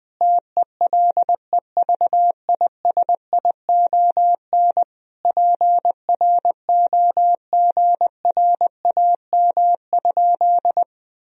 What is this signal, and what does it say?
Morse code, heard as a keyed tone, about 21 wpm, 700 Hz. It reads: TELEVISION PROGRAM?
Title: Sentences from Top 1000 Words - Review Part 4 - 20wpm